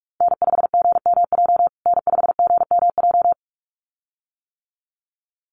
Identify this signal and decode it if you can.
Morse code, heard as a keyed tone, about 44 wpm, 700 Hz. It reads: N5GMJ N5GMJ
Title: USA Call Signs Repeated 2x 45wpm